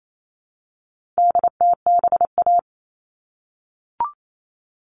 Morse code, heard as a keyed tone, about 28 wpm, 700 Hz. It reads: DTBA
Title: All Cut Numbers - 4-Digit 28wpm